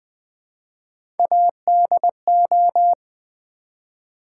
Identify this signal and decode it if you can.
Morse code, heard as a keyed tone, about 20 wpm, 700 Hz. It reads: ADO
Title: Build A Word - 1 Letter At A Time - 5-Letter Words - 20wpm